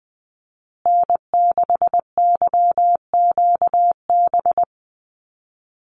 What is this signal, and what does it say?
Morse code, heard as a keyed tone, about 20 wpm, 700 Hz. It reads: N6YQB
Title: USA Call Signs Repeated 2x 20wpm